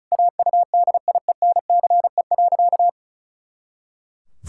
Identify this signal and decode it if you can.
Morse code, heard as a keyed tone, about 36 wpm, 700 Hz. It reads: AUDIENCE.
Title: Pangram Challenge v2 - 35wpm